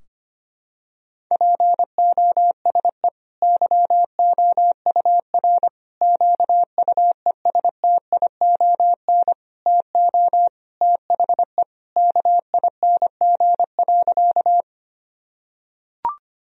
Morse code, heard as a keyed tone, about 25 wpm, 700 Hz. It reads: POSE YOUR QUESTION TO THE KING.
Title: Sentences from Top 1000 Words - Review Part 2 - 25wpm